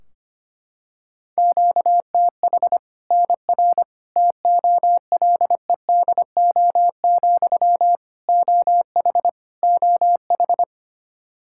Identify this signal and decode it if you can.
Morse code, heard as a keyed tone, about 25 wpm, 700 Hz. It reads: QTH NR TOLEDO, OH OH
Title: Practice QSO - Exchange 1 - Beginning - 25wpm